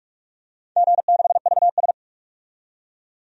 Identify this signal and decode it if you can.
Morse code, heard as a keyed tone, about 44 wpm, 690 Hz. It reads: G6VS
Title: DXCC-100 Call Signs - 4 Characters - 45wpm